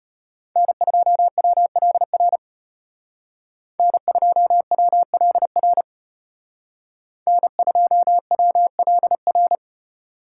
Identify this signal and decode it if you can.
Morse code, heard as a keyed tone, about 32 wpm, 700 Hz. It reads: N2WLR N2WLR N2WLR